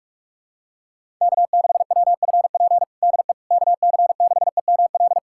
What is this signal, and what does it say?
Morse code, heard as a keyed tone, about 46 wpm, 700 Hz. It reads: K6WFP DE KC6EML
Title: USA Callsign DE Callsign 45wpm